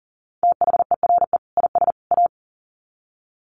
Morse code, heard as a keyed tone, about 40 wpm, 700 Hz. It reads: THERE IS A
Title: Sets of 3 Words 40wpm